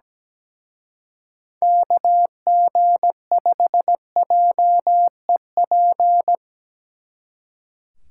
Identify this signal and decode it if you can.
Morse code, heard as a keyed tone, about 19 wpm, 700 Hz. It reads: KG5JEP
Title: Speed Racing - USA Call Signs 17wpm